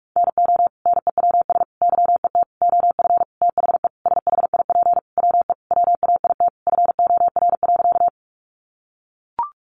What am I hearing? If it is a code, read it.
Morse code, heard as a keyed tone, about 44 wpm, 700 Hz. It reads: NO NEWS YET OF THE SHIP WE WAIT FOR.
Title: Sentences from Top 800 Words 45wpm